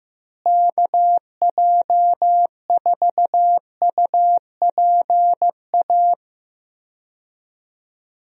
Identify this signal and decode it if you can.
Morse code, heard as a keyed tone, about 15 wpm, 700 Hz. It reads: KJ4UPA
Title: USA Call Signs - 6 Characters - Repeated 2x - 15wpm